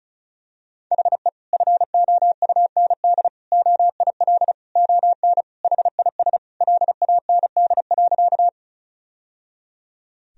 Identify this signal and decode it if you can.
Morse code, heard as a keyed tone, about 36 wpm, 700 Hz. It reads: HE FOUND OIL ON HIS LAND.
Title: Sentences from Top 1000 Words - Review Part 4 - 35wpm